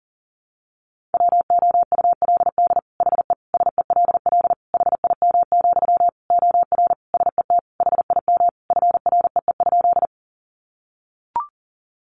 Morse code, heard as a keyed tone, about 40 wpm, 700 Hz. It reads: WOULD HE SELL HIM, OR SET HIM FREE?
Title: Sentences from Top 1100 Words 40wpm